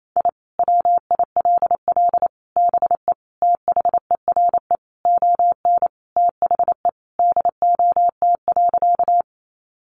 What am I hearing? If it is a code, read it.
Morse code, heard as a keyed tone, about 28 wpm, 700 Hz. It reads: I WILL BE THERE ON THE DOT.
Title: Sentences from Top 1500 Words 28wpm